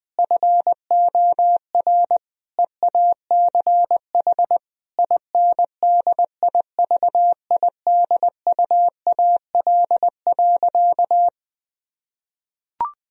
Morse code, heard as a keyed tone, about 21 wpm, 700 Hz. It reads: FOR EACH INDIVIDUAL.
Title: Sentences from Top 1000 Words - Review Part 4 - 20wpm